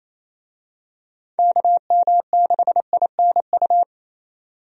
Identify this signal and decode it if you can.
Morse code, heard as a keyed tone, about 27 wpm, 700 Hz. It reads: KM6INU
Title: USA Call Signs - 6 Characters - 28wpm